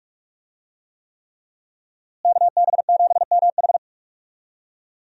Morse code, heard as a keyed tone, about 46 wpm, 700 Hz. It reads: KB7MH
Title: USA Call Signs - 5 Characters - Repeated 3x - 45wpm